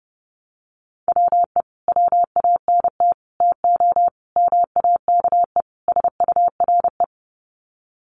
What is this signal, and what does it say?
Morse code, heard as a keyed tone, about 28 wpm, 700 Hz. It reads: WE WANT TO MAKE SURE
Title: Sets of 5 Words Encore2 30wpm